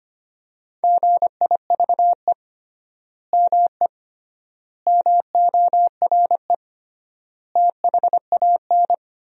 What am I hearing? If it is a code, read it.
Morse code, heard as a keyed tone, about 25 wpm, 700 Hz. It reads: GIVE ME MORE THAN
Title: 3x Word-Spacing - Sentences from Top 200 Words - 25wpm